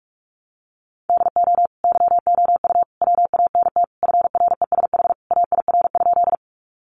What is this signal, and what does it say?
Morse code, heard as a keyed tone, about 44 wpm, 700 Hz. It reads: DO YOU WANT FRESH AIR?
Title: Sentences from Top 1200 Words 45wpm